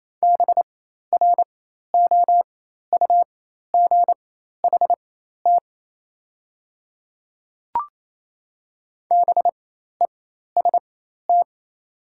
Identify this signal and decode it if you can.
Morse code, heard as a keyed tone, about 27 wpm, spaced as for 13 wpm, 700 Hz. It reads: BROUGHT BEST
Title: 4x Character Spacing - Top 500 Words - 28wpm